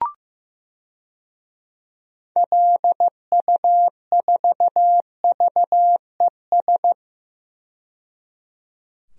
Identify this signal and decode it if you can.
Morse code, heard as a keyed tone, about 16 wpm, 700 Hz. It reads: LU4VES
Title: DXCC-100 Call Signs - 6 Characters - 15wpm